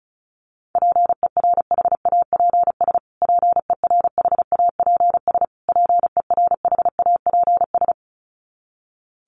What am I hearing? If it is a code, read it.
Morse code, heard as a keyed tone, about 34 wpm, 700 Hz. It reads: PERHAPS PERHAPS PERHAPS
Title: Top 500 Words Repeated 35wpm